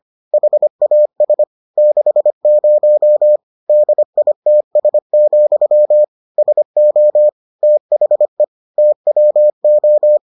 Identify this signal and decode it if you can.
Morse code, heard as a keyed tone, about 25 wpm, 600 Hz. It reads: HAS 60 DITS, SO THE TWO